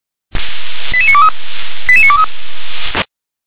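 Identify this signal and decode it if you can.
Five-tone selective call (ZVEI), heard as quick runs of five tones, tones of about 70 ms. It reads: 88923 88923
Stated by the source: ZVEI five-tone call received by SDR.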